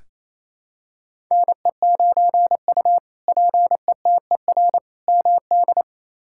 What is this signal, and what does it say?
Morse code, heard as a keyed tone, about 27 wpm, 700 Hz. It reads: NE9U PETER MD
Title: K1USN SST QSO Practice - 28wpm